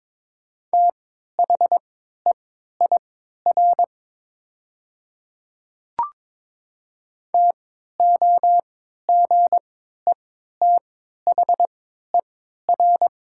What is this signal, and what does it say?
Morse code, heard as a keyed tone, about 22 wpm, spaced as for 12 wpm, 700 Hz. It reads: THEIR TOGETHER
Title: 3x Character Spacing - Top 500 Words - 22wpm